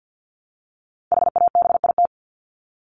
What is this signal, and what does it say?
Morse code, heard as a keyed tone, about 50 wpm, 700 Hz. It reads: HABIT
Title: Top 1751-2000 Words 50wpm